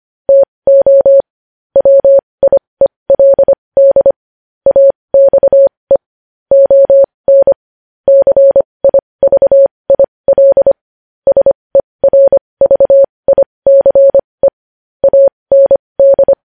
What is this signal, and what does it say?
Morse code, heard as a keyed tone, about 25 wpm, spaced as for 20 wpm, 550 Hz. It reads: TO WIELD AXE ON CIVIL SERVICE AND
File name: podcast / News headlines-1x-25x20